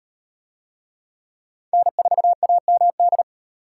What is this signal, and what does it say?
Morse code, heard as a keyed tone, about 36 wpm, 700 Hz. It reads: N4AMD